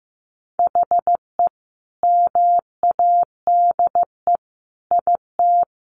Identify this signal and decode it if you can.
Morse code, heard as a keyed tone, about 15 wpm, 700 Hz. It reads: HE MADE IT
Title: Sentences from Top 1300 Words 15wpm